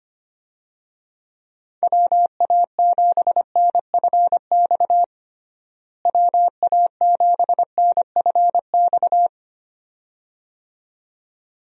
Morse code, heard as a keyed tone, about 25 wpm, 700 Hz. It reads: WA7NFX WA7NFX
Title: USA Call Signs - 6 Characters - Repeated 2x - 25wpm